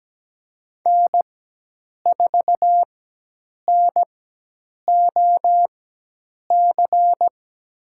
Morse code, heard as a keyed tone, about 17 wpm, spaced as for 8 wpm, 700 Hz. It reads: N4NOC
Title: USA Call Signs - 5 Characters - 4x Character Spacing - 17wpm